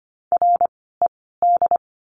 Morse code, heard as a keyed tone, about 25 wpm, spaced as for 16 wpm, 700 Hz. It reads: RED
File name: course / 25:10wpm / 094 Lesson 16f - Letter c - Words 25wpm